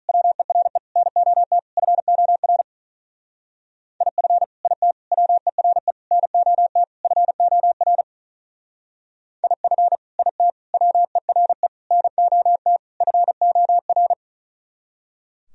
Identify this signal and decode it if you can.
Morse code, heard as a keyed tone, about 45 wpm, 700 Hz. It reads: WERE NOT FOR IF IT WERE NOT FOR IF IT WERE NOT FOR